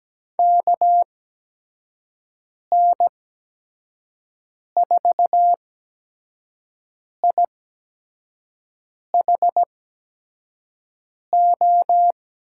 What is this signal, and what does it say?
Morse code, heard as a keyed tone, about 17 wpm, spaced as for 5 wpm, 700 Hz. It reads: KN4IHO
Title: USA Call Signs - 6 Characters - 8x Character Spacing - 17wpm